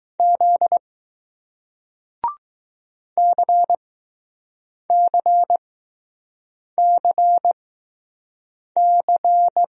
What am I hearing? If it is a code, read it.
Morse code, heard as a keyed tone, about 21 wpm, spaced as for 6 wpm, 700 Hz. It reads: Z CCCC